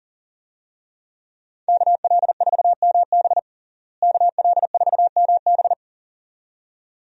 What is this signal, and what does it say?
Morse code, heard as a keyed tone, about 40 wpm, 700 Hz. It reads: KL4MB KL4MB
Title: USA Call Signs - 5 Characters - Repeated 2x - 40wpm